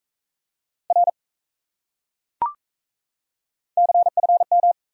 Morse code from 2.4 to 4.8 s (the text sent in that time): CFM